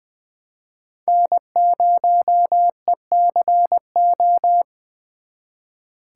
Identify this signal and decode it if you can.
Morse code, heard as a keyed tone, about 23 wpm, 700 Hz. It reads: N0ECO
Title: Speed Racing - USA 1x3 Call Signs 20wpm